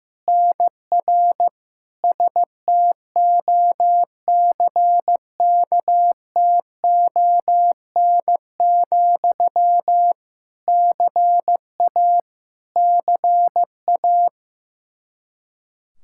Morse code, heard as a keyed tone, about 16 wpm, 700 Hz. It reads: NR STOCKTON, CA CA
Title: Practice QSO - Exchange 1 - Beginning - 15wpm